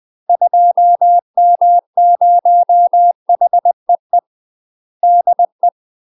Morse code, heard as a keyed tone, about 20 wpm, 700 Hz. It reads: 2M0HEE DE
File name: UK MASTER QSOs TRAINING / 20WPM / UK MASTER QSOs 0018